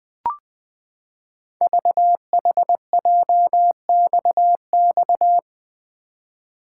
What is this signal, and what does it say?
Morse code, heard as a keyed tone, about 20 wpm, 700 Hz. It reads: VHJXX
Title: Code Groups - 5 Letters - 20wpm